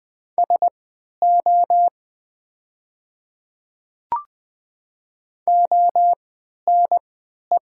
Morse code, heard as a keyed tone, about 20 wpm, spaced as for 11 wpm, 700 Hz. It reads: SO ONE
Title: 3x Character Spacing - Top 100 Words - 20wpm